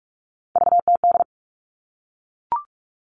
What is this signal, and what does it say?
Morse code, heard as a keyed tone, about 44 wpm, 700 Hz. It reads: VTD
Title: All Cut Numbers - 3-Digit 45wpm